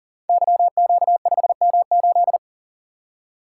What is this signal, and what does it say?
Morse code, heard as a keyed tone, about 39 wpm, 700 Hz. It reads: YQ5M8
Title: Code Groups - 5 Letter:Numbers - 40wpm